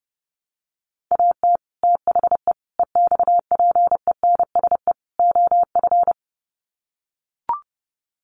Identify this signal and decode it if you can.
Morse code, heard as a keyed tone, about 30 wpm, 700 Hz. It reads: AT THE EXPENSE OF
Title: Sets of 4 Words 30wpm